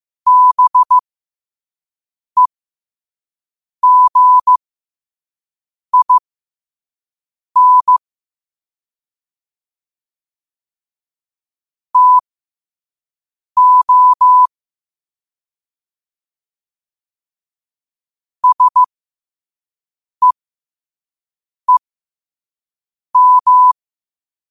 Morse code, heard as a keyed tone, about 15 wpm, spaced as for 5 wpm, 1000 Hz. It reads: BEGIN TO SEEM
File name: qotdfiles / 5wpm